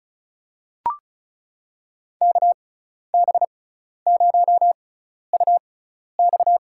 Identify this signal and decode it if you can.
Morse code, heard as a keyed tone, about 36 wpm, spaced as for 12 wpm, 700 Hz. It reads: KB0UX
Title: USA Call Signs - 5 Characters - 6x Character Spacing - 35wpm